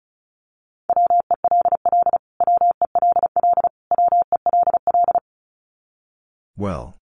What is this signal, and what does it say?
Morse code, heard as a keyed tone, about 34 wpm, 700 Hz. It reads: WELL WELL WELL
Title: Top 500 Words Repeated 35wpm